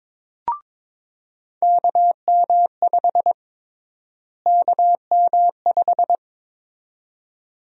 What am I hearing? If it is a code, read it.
Morse code, heard as a keyed tone, about 22 wpm, 700 Hz. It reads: KM5 KM5
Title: USA Call Sign Prefixes - Repeated 2x - 22wpm